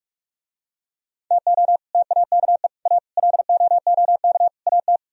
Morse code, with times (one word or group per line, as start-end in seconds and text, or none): 1.3-1.8 TO
1.9-2.7 TAKE
2.9-3.0 A
3.2-4.5 LOOK
4.7-5.0 AT